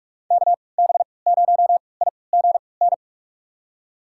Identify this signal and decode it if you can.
Morse code, heard as a keyed tone, about 42 wpm, spaced as for 25 wpm, 700 Hz. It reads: KB0IGN
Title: USA Call Signs - 3x Character Spacing - 45wpm